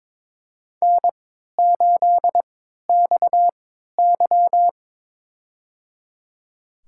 Morse code, heard as a keyed tone, about 22 wpm, spaced as for 12 wpm, 700 Hz. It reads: N8XY
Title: USA Call Signs - 4 Characters - 3x Character Spacing - 22wpm